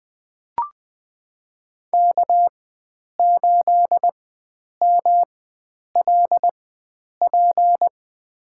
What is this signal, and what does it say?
Morse code, heard as a keyed tone, about 20 wpm, spaced as for 9 wpm, 700 Hz. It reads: K8MLP
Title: USA Call Signs - 5 Characters - 4x Character Spacing - 20wpm